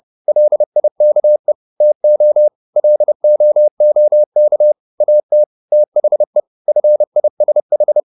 Morse code, heard as a keyed tone, about 30 wpm, 600 Hz. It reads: LIKE TO LOOK AT THE FISH